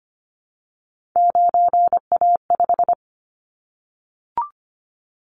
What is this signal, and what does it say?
Morse code, heard as a keyed tone, about 25 wpm, 700 Hz. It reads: 9A5
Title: Callsign Prefixes for Top 25 DXCC - 25wpm